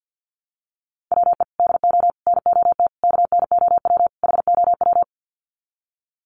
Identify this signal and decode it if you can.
Morse code, heard as a keyed tone, about 50 wpm, 700 Hz. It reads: WE DO NOT KNOW HOW
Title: Sets of 5 Words Encore 50wpm